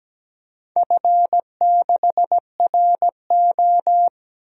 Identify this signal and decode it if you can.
Morse code, heard as a keyed tone, about 17 wpm, 700 Hz. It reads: F6RO
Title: DXCC-100 Call Signs - 4 Characters - 17wpm